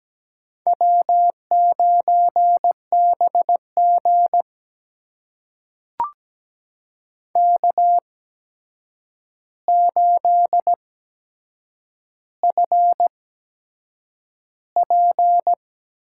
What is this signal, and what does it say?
Morse code, heard as a keyed tone, about 17 wpm, spaced as for 7 wpm, 700 Hz. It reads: W9BG K8FP
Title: USA Call Signs - 4 Characters - 8x Character Spacing - 17wpm